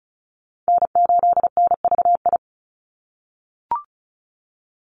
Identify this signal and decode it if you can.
Morse code, heard as a keyed tone, about 34 wpm, 700 Hz. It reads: N8NVI
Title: USA Call Signs 35wpm